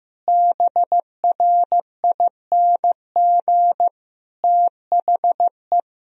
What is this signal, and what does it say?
Morse code, heard as a keyed tone, about 15 wpm, 700 Hz. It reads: BRING THE
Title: Sentences from Top 500 Words and Abbreviations 15wpm